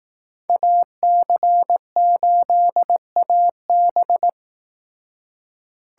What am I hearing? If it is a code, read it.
Morse code, heard as a keyed tone, about 18 wpm, 700 Hz. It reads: AC8AB
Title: Speed Racing - USA Call Signs 15wpm